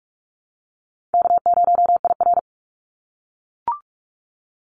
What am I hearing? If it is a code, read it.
Morse code, heard as a keyed tone, about 44 wpm, 700 Hz. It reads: K0IR
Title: Call Signs - 45wpm